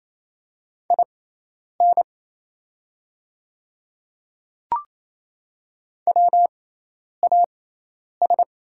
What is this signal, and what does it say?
Morse code, heard as a keyed tone, about 27 wpm, spaced as for 10 wpm, 690 Hz. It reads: IN WAS
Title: 6x Character Spacing - Top 100 Words - 28wpm